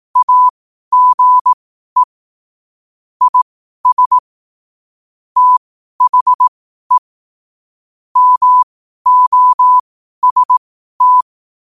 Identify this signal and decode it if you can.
Morse code, heard as a keyed tone, about 18 wpm, spaced as for 12 wpm, 1000 Hz. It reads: AGE IS THE MOST
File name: qotdfiles / 12wpm